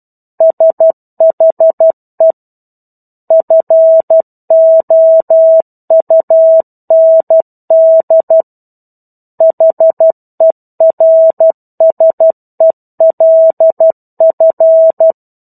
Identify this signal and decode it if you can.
Morse code, heard as a keyed tone, about 12 wpm, 650 Hz. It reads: SHE FOUND HERSELF